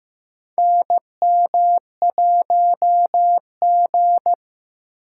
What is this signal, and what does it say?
Morse code, heard as a keyed tone, about 15 wpm, 700 Hz. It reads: NM1G
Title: USA Call Signs - 4 Characters - 4x Character Spacing - 15wpm